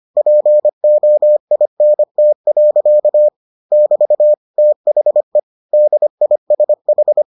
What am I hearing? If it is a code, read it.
Morse code, heard as a keyed tone, about 25 wpm, 600 Hz. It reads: POINT. <BT> THE DISH